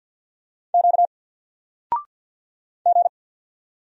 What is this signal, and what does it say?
Morse code, heard as a keyed tone, about 50 wpm, 700 Hz. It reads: Q G
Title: Single Character (Letter:Number:Punctuation:Prosign) 50wpm